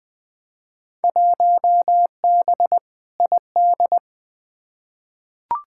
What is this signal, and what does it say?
Morse code, heard as a keyed tone, about 20 wpm, 700 Hz. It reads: 1B ID
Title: Field Day Exchanges - 20wpm